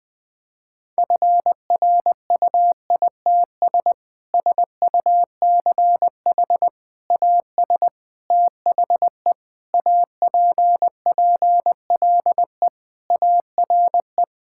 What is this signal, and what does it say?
Morse code, heard as a keyed tone, about 20 wpm, 700 Hz. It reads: FRUITS SUCH AS THE APPLE ARE
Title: Sentences with 3-Word Phrases - 20wpm